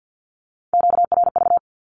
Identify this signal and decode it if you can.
Morse code, heard as a keyed tone, about 50 wpm, 700 Hz. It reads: QRV
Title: Q-Codes with Spoken Meaning - 50wpm